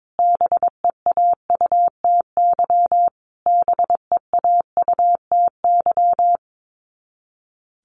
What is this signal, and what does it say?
Morse code, heard as a keyed tone, about 22 wpm, 700 Hz. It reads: BEAUTY BEAUTY
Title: Top 500 Words Repeated 22wpm